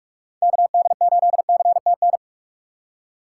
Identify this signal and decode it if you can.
Morse code, heard as a keyed tone, about 46 wpm, 700 Hz. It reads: KD8CTN
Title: USA Call Signs - 6 Characters - 2x Character Spacing - 45wpm